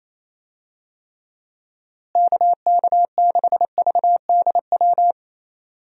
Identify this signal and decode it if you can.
Morse code, heard as a keyed tone, about 28 wpm, 700 Hz. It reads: KK6VDW
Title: USA Call Signs - 6 Characters - Repeated 3x - 28wpm